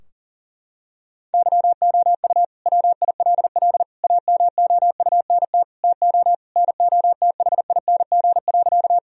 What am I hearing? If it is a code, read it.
Morse code, heard as a keyed tone, about 40 wpm, 700 Hz. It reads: YOU WILL AMOUNT TO NOTHING.
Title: Sentences from Top 1000 Words - Review Part 4 - 40wpm